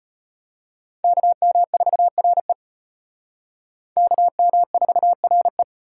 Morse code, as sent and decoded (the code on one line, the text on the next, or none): -.- -- ....- .-. / -.- -- ....- .-. .
KM4RE KM4RE